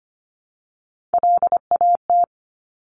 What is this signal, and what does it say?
Morse code, heard as a keyed tone, about 25 wpm, 700 Hz. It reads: LAT
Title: Top Three Letter Combinations in Words 25wpm